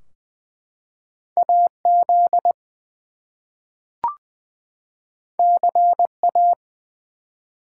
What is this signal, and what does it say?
Morse code, heard as a keyed tone, about 20 wpm, 700 Hz. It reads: AZ CA
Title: US States and Canadian Provinces 20wpm